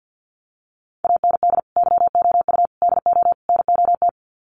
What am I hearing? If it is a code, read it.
Morse code, heard as a keyed tone, about 48 wpm, 700 Hz. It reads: AND YOU DO NOT
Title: Sets of 4 Words 50wpm